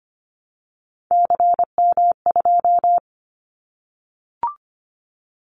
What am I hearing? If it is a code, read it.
Morse code, heard as a keyed tone, about 25 wpm, 700 Hz. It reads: CM2
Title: Callsign Prefixes for Top 100 DXCC - 25wpm